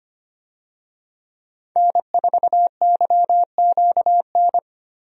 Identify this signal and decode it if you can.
Morse code, heard as a keyed tone, about 25 wpm, 700 Hz. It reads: N4YQN
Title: USA Call Signs - 5 Characters - Repeated 3x - 25wpm